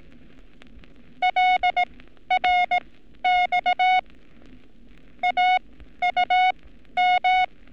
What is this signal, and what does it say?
Morse code, heard as a keyed tone, about 18 wpm, spaced as for 12 wpm, 730 Hz. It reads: LRX AUM
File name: training / NAVPERS 11224-11235 / NAVPERS 11227 RA RB